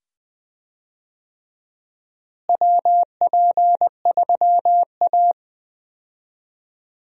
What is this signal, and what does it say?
Morse code, heard as a keyed tone, about 20 wpm, 700 Hz. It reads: WP3A
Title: USA Call Signs - 4 Characters - Repeated 2x - 20wpm